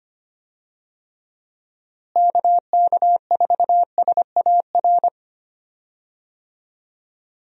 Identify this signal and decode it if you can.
Morse code, heard as a keyed tone, about 25 wpm, 700 Hz. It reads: KK4SAR
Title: USA Call Signs - 6 Characters - Repeated 3x - 25wpm